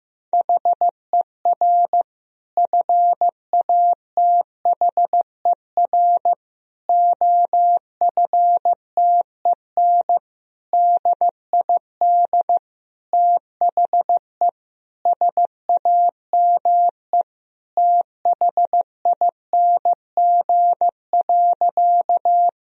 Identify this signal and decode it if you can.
Morse code, heard as a keyed tone, about 15 wpm, 700 Hz. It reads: HER FATHER OFTEN DID THE SAME THING.
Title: Sentences from Top 1000 Words - Review Part 1 - 15wpm